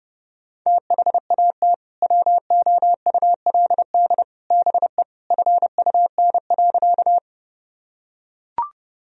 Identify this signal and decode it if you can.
Morse code, heard as a keyed tone, about 30 wpm, 700 Hz. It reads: THAT WOULD BE FUN.
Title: Sentences from Top 1000 Words - Review Part 4 - 30wpm